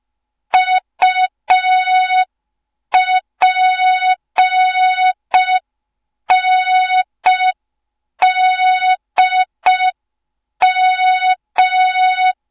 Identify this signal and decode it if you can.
Morse code, heard as a keyed tone, about 5 wpm, 3700 Hz. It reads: UPNDM
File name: Morze-HA2MN / hang / 07 GYAKORLAT